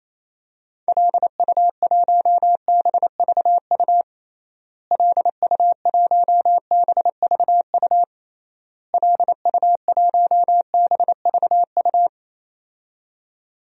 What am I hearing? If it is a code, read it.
Morse code, heard as a keyed tone, about 28 wpm, 700 Hz. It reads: LU1BVU LU1BVU LU1BVU
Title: DXCC-100 Call Signs - 6 Characters - Repeated 3x - 28wpm